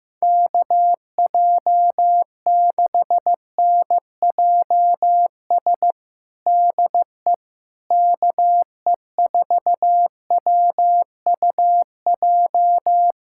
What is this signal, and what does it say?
Morse code, heard as a keyed tone, about 15 wpm, 700 Hz. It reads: KJ6NJS DE KE4WUJ
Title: USA Callsign DE Callsign 15wpm